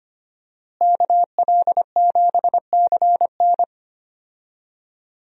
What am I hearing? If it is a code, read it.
Morse code, heard as a keyed tone, about 25 wpm, 700 Hz. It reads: KL7CN
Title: USA Call Signs - 5 Characters - Repeated 2x - 25wpm